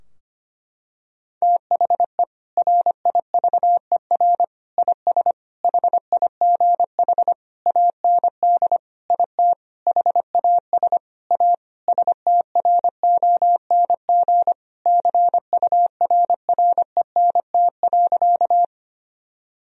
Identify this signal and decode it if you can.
Morse code, heard as a keyed tone, about 25 wpm, 700 Hz. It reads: THE RIVER IS HIGH AND IT HAS A STRONG CURRENT.
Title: Sentences from Top 1000 Words - Review Part 4 - 25wpm